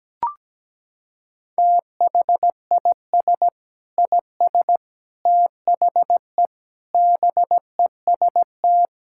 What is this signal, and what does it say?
Morse code, heard as a keyed tone, about 17 wpm, 700 Hz. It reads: THIS IS THE BEST
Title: Sentences from Top 1000 Words - Review Part 6 - 17wpm